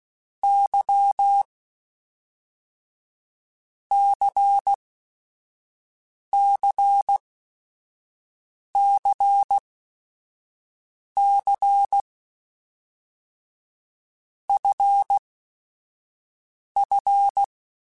Morse code, heard as a keyed tone, about 16 wpm, 800 Hz. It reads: Y C C C C F F
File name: Basic Sets / Basic Letter Groups Collective7 / LFCYQ@5